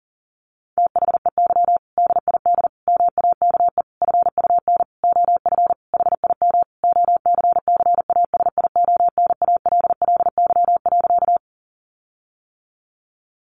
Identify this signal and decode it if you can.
Morse code, heard as a keyed tone, about 40 wpm, 700 Hz. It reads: THEY DID MAKE FUN OF HIM OCCASIONALLY.
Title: Sentences from Top 1300 Words 40wpm